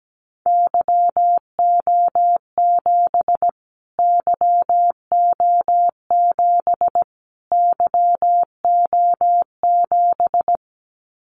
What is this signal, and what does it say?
Morse code, heard as a keyed tone, about 17 wpm, 700 Hz. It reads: YO7 YO7 YO7
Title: Callsign Prefixes for Top 25 DXCC - Repeated 3x - 17wpm